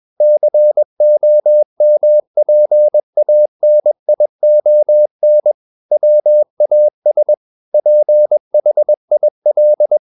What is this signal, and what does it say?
Morse code, heard as a keyed tone, about 21 wpm, 600 Hz. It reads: COMPANION WAS PHIL
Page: CW med Gnister